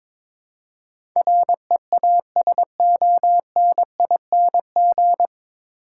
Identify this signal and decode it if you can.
Morse code, heard as a keyed tone, about 21 wpm, 700 Hz. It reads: REASONING
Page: Morse Code Ninja practice 22wpm